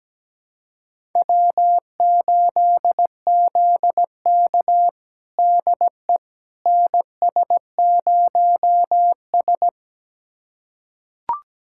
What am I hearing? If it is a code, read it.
Morse code, heard as a keyed tone, about 17 wpm, 700 Hz. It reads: W8ZK DE NS0S
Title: USA Callsign DE Callsign - 4 Characters - 17wpm